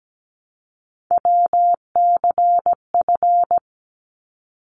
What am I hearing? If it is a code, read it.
Morse code, heard as a keyed tone, about 17 wpm, 700 Hz. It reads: WCF
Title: ARRL Sections 17wpm